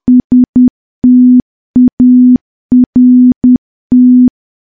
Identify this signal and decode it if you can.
Morse code, heard as a keyed tone, about 10 wpm, 250 Hz. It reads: START
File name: morse / morse-training-10WPM